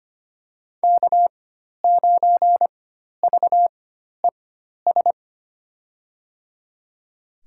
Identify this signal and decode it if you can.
Morse code, heard as a keyed tone, about 25 wpm, spaced as for 12 wpm, 700 Hz. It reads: K9VES
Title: USA Call Signs - 5 Characters - 4x Character Spacing - 25wpm